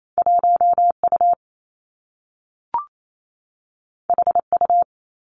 Morse code, heard as a keyed tone, about 28 wpm, 700 Hz. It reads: 1 U H U
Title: Two Challenging Letter-Number Combinations 28wpm